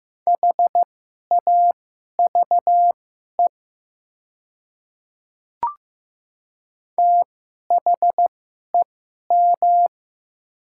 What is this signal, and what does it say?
Morse code, heard as a keyed tone, about 15 wpm, spaced as for 11 wpm, 700 Hz. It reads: HAVE THEM
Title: 2x Character Spacing - Top 100 Words - 15wpm